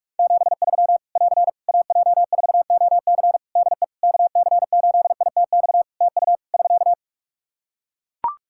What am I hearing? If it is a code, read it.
Morse code, heard as a keyed tone, about 41 wpm, 700 Hz. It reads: 73 <AR> AJ4OC DE KC8ITX TU <SK>
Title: Practice QSO - Exchange 4 - Ending - 45wpm